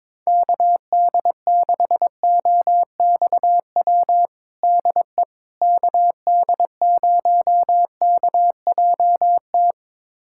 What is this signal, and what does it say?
Morse code, heard as a keyed tone, about 21 wpm, 700 Hz. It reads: KD6OXW DE KD0KJT
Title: USA Callsign DE Callsign - 6 Characters - 22wpm